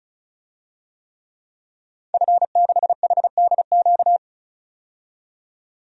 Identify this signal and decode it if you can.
Morse code, heard as a keyed tone, about 34 wpm, 700 Hz. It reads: F6HDQ
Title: DXCC-100 Call Signs - 5 Characters - 35wpm